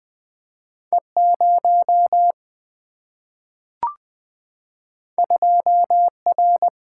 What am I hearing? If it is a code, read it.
Morse code, heard as a keyed tone, about 20 wpm, 700 Hz. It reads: E0 2R
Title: Two Letter-Number Combinations - 20wpm